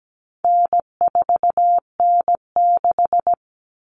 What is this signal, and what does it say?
Morse code, heard as a keyed tone, about 17 wpm, 700 Hz. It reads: N4N6
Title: Common Cut Numbers - 4-Digit 17wpm